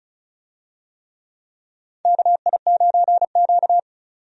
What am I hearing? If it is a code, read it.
Morse code, heard as a keyed tone, about 36 wpm, 700 Hz. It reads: KI9Q
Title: USA Call Signs - 4 Characters - 35wpm